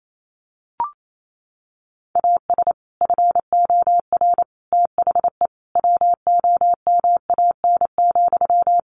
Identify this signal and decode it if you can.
Morse code, heard as a keyed tone, about 28 wpm, 700 Hz. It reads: AS FOR THE WOMAN,
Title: Sentences from Top 1100 Words 28wpm